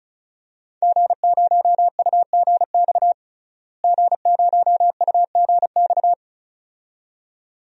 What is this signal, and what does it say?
Morse code, heard as a keyed tone, about 34 wpm, 700 Hz. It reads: G0UGX G0UGX
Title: DXCC-100 Call Signs - 5 Characters - Repeated 2x - 35wpm